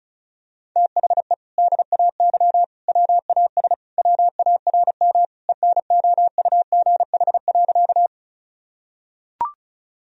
Morse code, heard as a keyed tone, about 34 wpm, 700 Hz. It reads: THE DAY WAS WARM ENOUGH.
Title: Sentences from Top 1000 Words - Review Part 2 - 35wpm